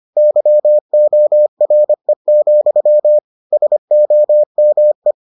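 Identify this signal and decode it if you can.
Morse code, heard as a keyed tone, about 25 wpm, 600 Hz. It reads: YORE, SOME
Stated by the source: Morse edition of the Foundations of Amateur Radio episodes.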